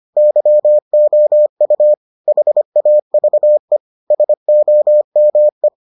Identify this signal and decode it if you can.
Morse code, heard as a keyed tone, about 25 wpm, 600 Hz. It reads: YOU HAVE SOME